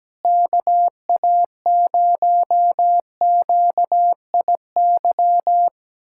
Morse code, heard as a keyed tone, about 17 wpm, 700 Hz. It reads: KA0QIY
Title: USA Callsign DE Callsign - 6 Characters - 17wpm